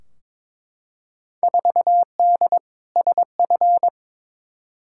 Morse code, heard as a keyed tone, about 22 wpm, 700 Hz. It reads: 4D SF
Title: Field Day Exchanges Encore - 22wpm